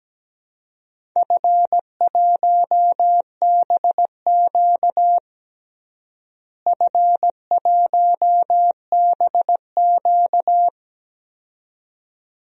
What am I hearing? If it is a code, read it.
Morse code, heard as a keyed tone, about 17 wpm, 700 Hz. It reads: F1BQ F1BQ
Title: DXCC-100 Call Signs - 4 Characters - Repeated 2x - 17wpm